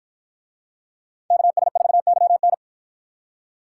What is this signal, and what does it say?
Morse code, heard as a keyed tone, about 49 wpm, 700 Hz. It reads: KS4YN